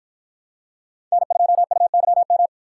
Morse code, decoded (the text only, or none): N2UYM